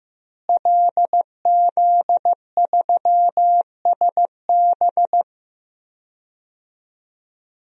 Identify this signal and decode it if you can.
Morse code, heard as a keyed tone, about 15 wpm, 700 Hz. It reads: LZ3SB
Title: DXCC-100 Call Signs - 5 Characters - Repeated 3x - 15wpm